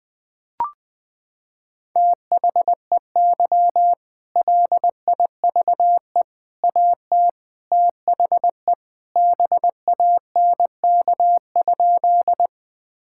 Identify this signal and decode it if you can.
Morse code, heard as a keyed tone, about 20 wpm, 700 Hz. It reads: THEY LIVE AT THE BANK?
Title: Sentences from Top 1000 Words - Review Part 6 - 20wpm